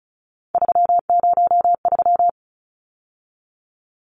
Morse code, heard as a keyed tone, about 34 wpm, 700 Hz. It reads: 303
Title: Numbers - 3-Digit 35wpm